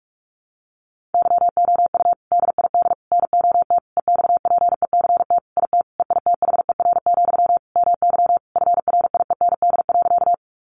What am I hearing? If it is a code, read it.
Morse code, heard as a keyed tone, about 44 wpm, 700 Hz. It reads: YOU DID NOT EXPECT IT EITHER, MY FRIEND.
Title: Sentences from Top 1100 Words 45wpm